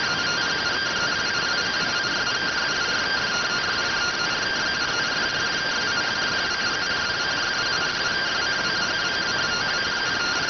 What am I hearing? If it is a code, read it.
Two-tone teleprinter signal, about 170 Hz shift.